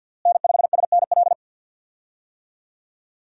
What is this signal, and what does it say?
Morse code, heard as a keyed tone, about 53 wpm, 700 Hz. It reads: N5SNL
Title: USA Call Signs - 5 Characters - Repeated 2x - 50wpm